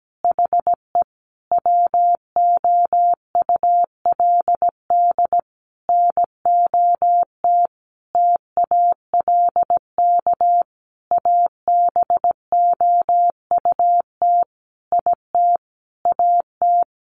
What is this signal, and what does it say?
Morse code, heard as a keyed tone, about 17 wpm, 700 Hz. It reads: HE WOULD NOT TALK ABOUT IT AT
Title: Sentences from Top 800 Words 17wpm